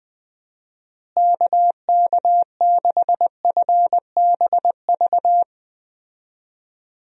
Morse code, heard as a keyed tone, about 20 wpm, 700 Hz. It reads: KK6FBV
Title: USA Call Signs - 6 Characters - Repeated 3x - 20wpm